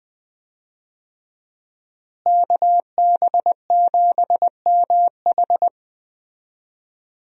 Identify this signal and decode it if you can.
Morse code, heard as a keyed tone, about 20 wpm, 700 Hz. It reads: KB7MH
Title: USA Call Signs - 5 Characters - Repeated 3x - 20wpm